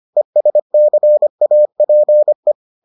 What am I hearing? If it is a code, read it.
Morse code, heard as a keyed tone, about 25 wpm, 600 Hz. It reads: ESCAPE